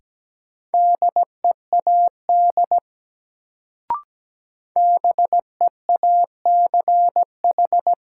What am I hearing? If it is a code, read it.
Morse code, heard as a keyed tone, about 17 wpm, 700 Hz. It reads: DEAD BEACH
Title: Sentences from Top 1500 Words - New Word Drill - 17wpm